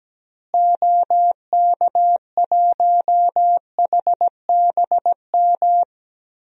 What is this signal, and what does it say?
Morse code, heard as a keyed tone, about 17 wpm, 700 Hz. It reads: OK1HBM
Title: DXCC-100 Call Signs - 6 Characters - Repeated 2x - 17wpm